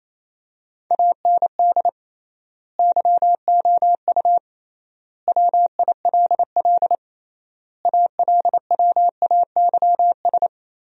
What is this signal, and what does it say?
Morse code, heard as a keyed tone, about 28 wpm, 700 Hz. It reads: AND YOU WILL ALWAYS
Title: 3x Word-Spacing - Sentences from Top 400 Words - 28wpm